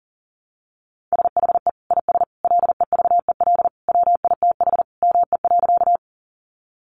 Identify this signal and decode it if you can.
Morse code, heard as a keyed tone, about 40 wpm, 700 Hz. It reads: SHE IS LEVEL WITH ME.
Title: Sentences from Top 700 Words 40wpm